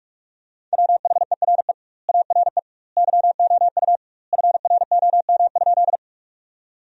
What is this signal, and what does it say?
Morse code, heard as a keyed tone, about 46 wpm, 700 Hz. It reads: WHERE ARE YOU FROM?
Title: 2x Word-Spacing - Sentences from Top 200 Words - 45wpm